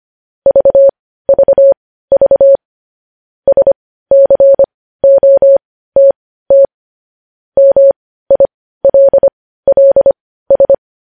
Morse code, heard as a keyed tone, about 25 wpm, spaced as for 15 wpm, 550 Hz. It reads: VVV SCOTT MILLS
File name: podcast / News headlines-1x-25x15